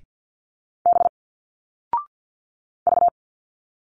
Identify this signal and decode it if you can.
Morse code, heard as a keyed tone, about 50 wpm, 700 Hz. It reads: B V
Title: Single Letters 50wpm